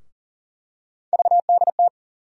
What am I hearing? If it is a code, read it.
Morse code, heard as a keyed tone, about 40 wpm, 700 Hz. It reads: VDT
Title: USA Call Sign Postfixes - 3 Letters 40wpm